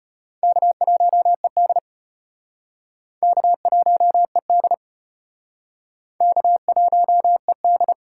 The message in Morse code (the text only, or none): K1ED K1ED K1ED